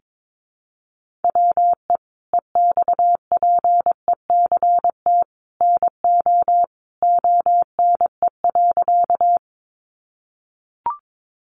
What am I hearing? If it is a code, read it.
Morse code, heard as a keyed tone, about 22 wpm, 700 Hz. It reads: WE EXPECT NO ONE.
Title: Sentences from Top 1100 Words 22wpm